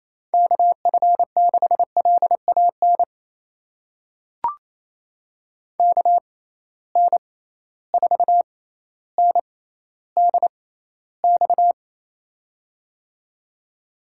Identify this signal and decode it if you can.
Morse code, heard as a keyed tone, about 27 wpm, spaced as for 14 wpm, 700 Hz. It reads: KF6LAN KN4NDX
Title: USA Call Signs - 6 Characters - 6x Character Spacing - 28wpm